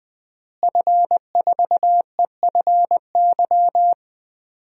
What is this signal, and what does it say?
Morse code, heard as a keyed tone, about 20 wpm, 700 Hz. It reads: F4EFY
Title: DXCC-100 Call Signs - 5 Characters - Repeated 3x - 20wpm